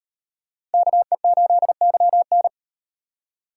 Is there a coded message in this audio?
Yes: Morse code from 0.7 to 2.5 s, heard as a keyed tone, about 36 wpm, 700 Hz. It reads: KE8YN